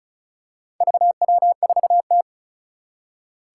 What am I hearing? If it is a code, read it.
Morse code, heard as a keyed tone, about 36 wpm, 700 Hz. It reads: VW4T
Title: Four Challenging Letter-Number Combinations 35wpm